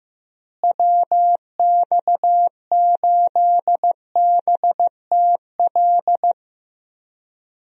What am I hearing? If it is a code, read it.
Morse code, heard as a keyed tone, about 15 wpm, 700 Hz. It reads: WX8BTL
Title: USA Call Signs - 6 Characters - Repeated 3x - 15wpm